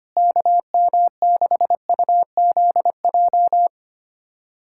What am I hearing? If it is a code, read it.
Morse code, heard as a keyed tone, about 25 wpm, 700 Hz. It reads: KM6UZJ
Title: USA Callsign DE Callsign - 6 Characters - 25wpm